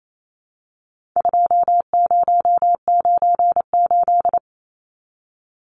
Morse code, heard as a keyed tone, about 28 wpm, 700 Hz. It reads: 2098
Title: Numbers - 4-Digit 28wpm